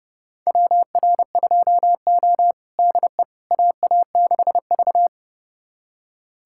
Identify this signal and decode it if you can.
Morse code, heard as a keyed tone, about 30 wpm, 700 Hz. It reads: WR2O DE AA6V
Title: USA Callsign DE Callsign - 4 Characters - 30wpm